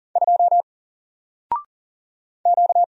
Morse code, heard as a keyed tone, about 40 wpm, 700 Hz. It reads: 2 Q
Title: Single Letter-Number Alt 40wpm